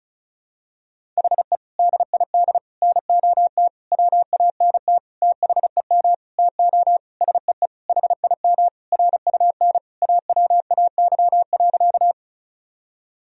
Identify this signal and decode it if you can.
Morse code, heard as a keyed tone, about 34 wpm, 700 Hz. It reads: HE DID NOT WANT THEM TO SEE HIM RUN AWAY.
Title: Sentences from Top 1000 Words - Review Part 1 - 35wpm